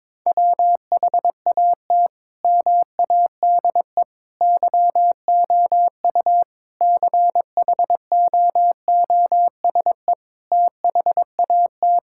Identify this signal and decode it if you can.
Morse code, heard as a keyed tone, about 22 wpm, 700 Hz. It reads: WHAT MADE YOU CHOOSE THAT
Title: Sentences from Top 1000 Words - Review Part 5 - 22wpm